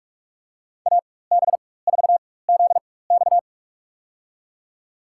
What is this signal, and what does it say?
Morse code, heard as a keyed tone, about 42 wpm, spaced as for 21 wpm, 700 Hz. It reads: AB4ZX
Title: USA Call Signs - 5 Characters - 4x Character Spacing - 45wpm